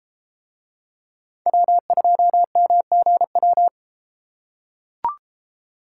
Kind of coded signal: Morse code

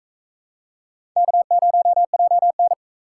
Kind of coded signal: Morse code

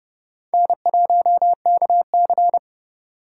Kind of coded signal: Morse code

